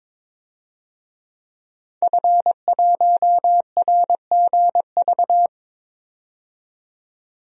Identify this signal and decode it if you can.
Morse code, heard as a keyed tone, about 22 wpm, 700 Hz. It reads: F1RGV
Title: DXCC-100 Call Signs - 5 Characters - 22wpm